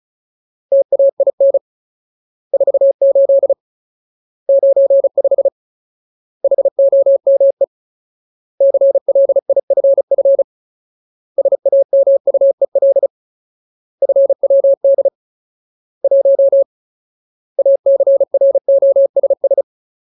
Morse code, heard as a keyed tone, about 34 wpm, 550 Hz. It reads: TAIN 48 95 HOME CLIFF SAMUEL FWD 1 ACROSS
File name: Ultimate Word List / 35WPM Ultimate